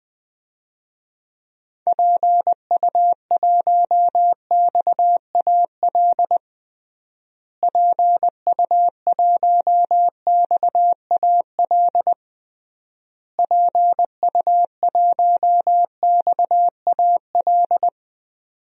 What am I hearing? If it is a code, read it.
Morse code, heard as a keyed tone, about 20 wpm, 700 Hz. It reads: PU1XAL PU1XAL PU1XAL
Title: DXCC-100 Call Signs - 6 Characters - Repeated 3x - 20wpm